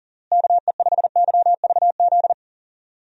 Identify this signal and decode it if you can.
Morse code, heard as a keyed tone, about 40 wpm, 700 Hz. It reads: KE5YVZ
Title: USA Call Signs - 6 Characters - Repeated 3x - 40wpm